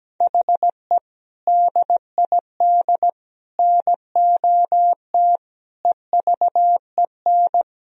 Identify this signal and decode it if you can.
Morse code, heard as a keyed tone, about 17 wpm, 700 Hz. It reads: HE DID NOT EVEN
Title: Sentences from Top 1000 Words - Review Part 1 - 17wpm